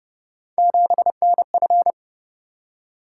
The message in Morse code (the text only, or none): 7NF